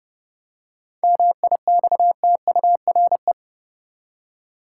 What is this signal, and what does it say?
Morse code, heard as a keyed tone, about 30 wpm, 700 Hz. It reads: MIXTURE
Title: Seven Letter Words - 30wpm